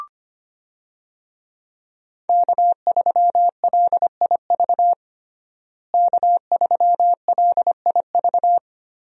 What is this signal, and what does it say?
Morse code, heard as a keyed tone, about 25 wpm, 700 Hz. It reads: K3LIV K3LIV
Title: USA Call Signs - 5 Characters - Repeated 3x - 25wpm